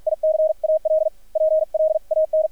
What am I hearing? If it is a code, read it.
Morse code, heard as a keyed tone, about 50 wpm, spaced as for 43 wpm, 630 Hz, with radio noise. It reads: ICAL PLAN
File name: archived / CRSnet